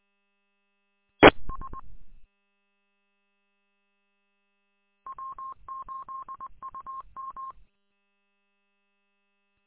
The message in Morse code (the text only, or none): ES W8UM